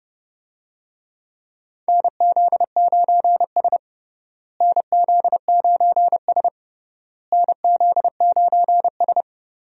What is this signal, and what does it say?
Morse code, heard as a keyed tone, about 31 wpm, 700 Hz. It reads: NZ9S NZ9S NZ9S
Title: USA Call Signs - 4 Characters - Repeated 3x - 30wpm